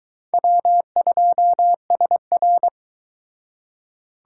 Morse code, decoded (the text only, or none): W2SR